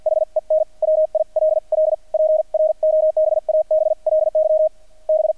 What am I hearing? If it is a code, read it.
Morse code, heard as a keyed tone, about 48 wpm, spaced as for 41 wpm, 630 Hz, with radio noise. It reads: HEN WILL PROBABLY B
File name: archived / CRSnet